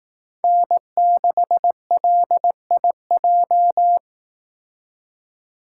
Morse code, heard as a keyed tone, about 18 wpm, 700 Hz. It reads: N6LIJ
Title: Speed Racing - USA 1x3 Call Signs 15wpm